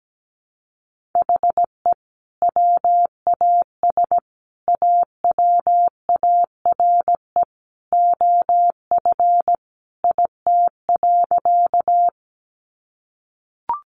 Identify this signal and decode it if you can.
Morse code, heard as a keyed tone, about 17 wpm, 700 Hz. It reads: HE WAS AWARE OF IT.
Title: Sentences from Top 1200 Words 17wpm